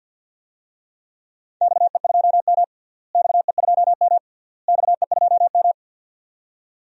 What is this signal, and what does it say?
Morse code, heard as a keyed tone, about 47 wpm, 700 Hz. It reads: XE2M XE2M XE2M
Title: DXCC-100 Call Signs - 4 Characters - Repeated 3x - 50wpm